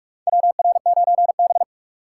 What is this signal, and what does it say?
Morse code, heard as a keyed tone, about 46 wpm, 700 Hz. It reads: WR9B
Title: USA Call Signs - 2x Character Spacing - 45wpm